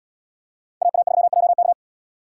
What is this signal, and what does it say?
Morse code, heard as a keyed tone, about 77 wpm, 700 Hz. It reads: AA3JK